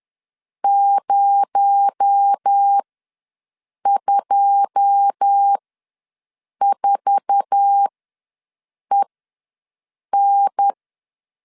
Morse code, heard as a keyed tone, about 10 wpm, spaced as for 6 wpm, 800 Hz. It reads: 024EN
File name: LEZIONI / 9 2 ws 10wpm 4130